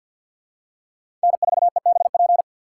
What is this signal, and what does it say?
Morse code, heard as a keyed tone, about 51 wpm, 700 Hz. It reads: N4EBP
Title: USA Call Signs - 5 Characters - 2x Character Spacing - 50wpm